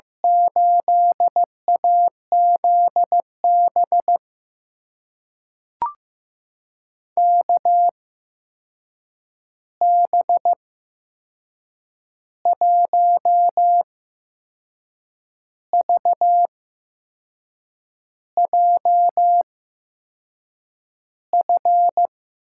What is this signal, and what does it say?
Morse code, heard as a keyed tone, about 15 wpm, spaced as for 6 wpm, 700 Hz. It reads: 8AZB KB1VJF
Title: USA Call Signs - 6 Characters - 8x Character Spacing - 15wpm